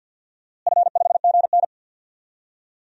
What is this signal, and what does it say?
Morse code, heard as a keyed tone, about 51 wpm, 700 Hz. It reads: F5GN
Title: DXCC-100 Call Signs - 4 Characters - Repeated 3x - 50wpm